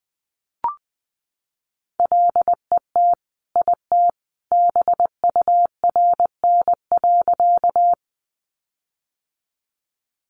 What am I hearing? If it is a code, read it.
Morse code, heard as a keyed tone, about 20 wpm, 700 Hz. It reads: LET IT BURN.
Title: Sentences from Top 1300 Words 20wpm